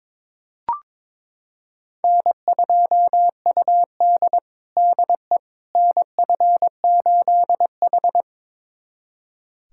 Morse code, heard as a keyed tone, about 22 wpm, 700 Hz. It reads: N2UD DE NF8H
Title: USA Callsign DE Callsign - 4 Characters - 22wpm